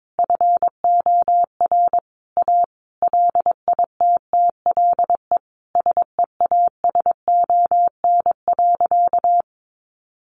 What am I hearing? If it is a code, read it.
Morse code, heard as a keyed tone, about 22 wpm, 700 Hz. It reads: FOR A LITTLE SEASON.
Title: Sentences from Top 1400 Words 22wpm